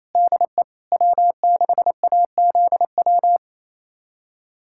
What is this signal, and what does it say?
Morse code, heard as a keyed tone, about 27 wpm, 690 Hz. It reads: DE W6AZW
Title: USA Callsign DE Callsign - 5 Characters - 28wpm